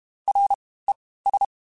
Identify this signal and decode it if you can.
Morse code, heard as a keyed tone, about 32 wpm, spaced as for 18 wpm, 800 Hz. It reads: RES
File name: Groups@30 / Groups@30 mp3s / VB46@30